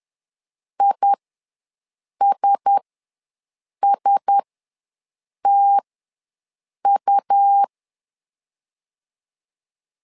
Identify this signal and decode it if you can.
Morse code, heard as a keyed tone, about 10 wpm, spaced as for 6 wpm, 800 Hz. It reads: ISSTU
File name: LEZIONI / 4 Q1S ws 10wpm 2208